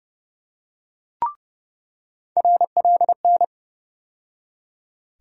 Morse code, heard as a keyed tone, about 30 wpm, 700 Hz. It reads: RLN
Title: USA Call Sign Postfixes - 3 Letters 30wpm